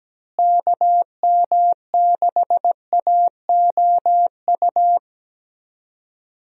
Morse code, heard as a keyed tone, about 17 wpm, 700 Hz. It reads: KM6AOU
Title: USA Call Signs - 6 Characters - Repeated 3x - 17wpm